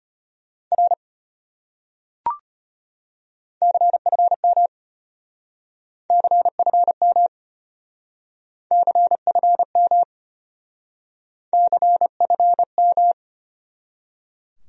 Morse code, heard as a keyed tone, about 33 wpm, 700 Hz. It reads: R CFM CFM CFM CFM